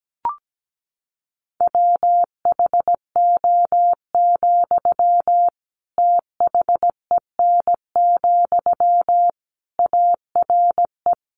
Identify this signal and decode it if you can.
Morse code, heard as a keyed tone, about 17 wpm, 700 Hz. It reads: WHO, THEN, ARE
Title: Sentences from Top 100 Words 17wpm